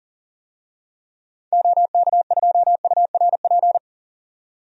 Morse code, heard as a keyed tone, about 40 wpm, 700 Hz. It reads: OK2URP
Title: DXCC-100 Call Signs - 6 Characters - 40wpm